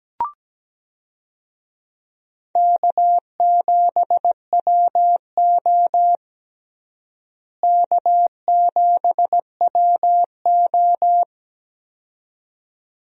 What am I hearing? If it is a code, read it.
Morse code, heard as a keyed tone, about 17 wpm, 700 Hz. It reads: K7WO K7WO
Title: USA Call Signs - 4 Characters - Repeated 2x - 17wpm